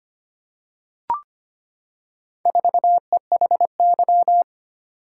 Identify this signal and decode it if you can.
Morse code, heard as a keyed tone, about 25 wpm, 700 Hz. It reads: 4EHY
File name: practice / 25wpm / Four Letter-Number Combinations - 25wpm